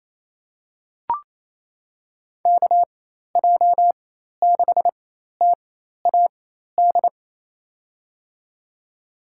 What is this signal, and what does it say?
Morse code, heard as a keyed tone, about 28 wpm, spaced as for 13 wpm, 700 Hz. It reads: KJ6TAD
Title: USA Call Signs - 6 Characters - 4x Character Spacing - 28wpm